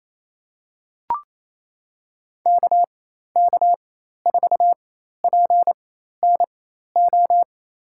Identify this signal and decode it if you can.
Morse code, heard as a keyed tone, about 28 wpm, spaced as for 13 wpm, 700 Hz. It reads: KK4PNO
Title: USA Call Signs - 6 Characters - 4x Character Spacing - 28wpm